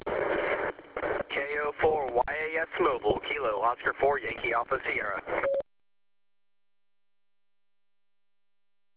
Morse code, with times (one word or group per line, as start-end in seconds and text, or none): none